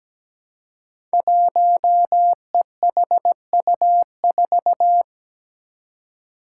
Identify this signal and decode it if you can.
Morse code, heard as a keyed tone, about 17 wpm, 700 Hz. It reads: 1EHU4
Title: Five Challenging Letter-Number Combinations 17wpm